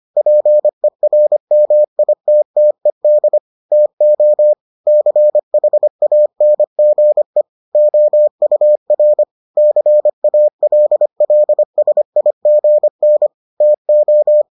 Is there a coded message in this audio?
Yes: Morse code from 0.2 to 14.4 s, heard as a keyed tone, about 25 wpm, 600 Hz. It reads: PERMITTED TO CHANGE OUR CALLSIGN TO